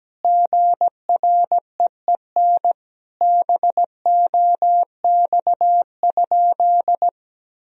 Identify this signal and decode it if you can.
Morse code, heard as a keyed tone, about 17 wpm, 700 Hz. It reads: GREEN BOX?
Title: Sentences from Top 1000 Words - Review Part 4 - 17wpm